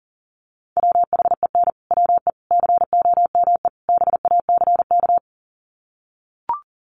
Morse code, heard as a keyed tone, about 40 wpm, 700 Hz. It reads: WHEN WE COME BACK
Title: Sets of 4 Words 40wpm